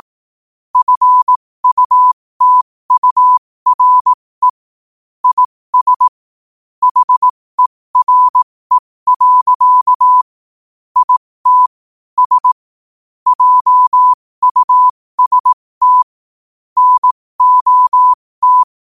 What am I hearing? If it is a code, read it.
Morse code, heard as a keyed tone, about 18 wpm, spaced as for 15 wpm, 1000 Hz. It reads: FUTURE IS HERE. IT S JUST NOT
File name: qotdfiles / 15wpm